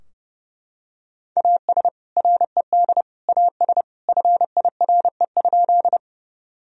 Morse code, heard as a keyed tone, about 30 wpm, 700 Hz. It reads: AS RED AS FIRE?
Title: Sentences from Top 1000 Words - Review Part 2 - 30wpm